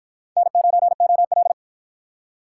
Morse code, heard as a keyed tone, about 45 wpm, 700 Hz. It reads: N9OL